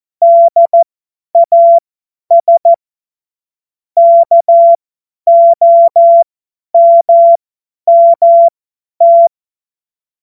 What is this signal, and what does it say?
Morse code, heard as a keyed tone, about 14 wpm, spaced as for 10 wpm, 680 Hz. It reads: DAS KOMMT
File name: Morsekurs / Witze / Witz 16. Dezember 2025 50 F70